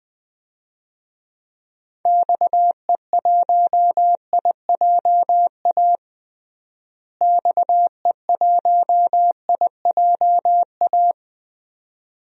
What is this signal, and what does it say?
Morse code, heard as a keyed tone, about 21 wpm, 700 Hz. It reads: XE1IJA XE1IJA
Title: DXCC-100 Call Signs - 6 Characters - Repeated 2x - 20wpm